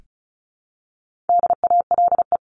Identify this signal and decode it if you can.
Morse code, heard as a keyed tone, about 34 wpm, 700 Hz. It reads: DALE
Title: CWT Exchange 35wpm